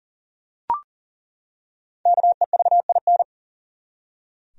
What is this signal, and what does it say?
Morse code, heard as a keyed tone, about 40 wpm, 700 Hz. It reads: KEVIN
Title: USA Common Names - Max 5 Letters - 40wpm